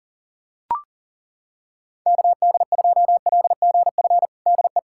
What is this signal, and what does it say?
Morse code, heard as a keyed tone, about 40 wpm, 700 Hz. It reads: KD2LGF DE
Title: USA Callsign DE Callsign - 6 Characters - 40wpm